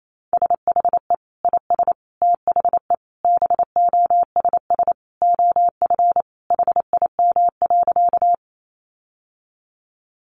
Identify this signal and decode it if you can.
Morse code, heard as a keyed tone, about 28 wpm, 700 Hz. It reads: SHE IS THE BOSS OF HIM.
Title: Sentences from Top 700 Words 28wpm